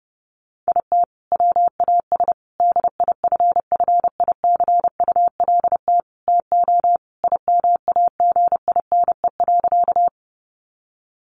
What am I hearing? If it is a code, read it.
Morse code, heard as a keyed tone, about 30 wpm, 700 Hz. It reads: IT WAS DIFFICULT TO IMAGINE.
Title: Sentences from Top 1400 Words 30wpm